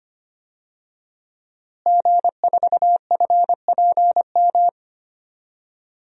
Morse code, heard as a keyed tone, about 25 wpm, 700 Hz. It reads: G4FPM
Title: DXCC-100 Call Signs - 5 Characters - Repeated 2x - 25wpm